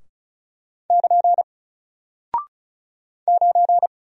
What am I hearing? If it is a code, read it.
Morse code, heard as a keyed tone, about 36 wpm, 700 Hz. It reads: <KN> 9
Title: Single Character (Letter:Number:Punctuation:Prosign) Alt 35wpm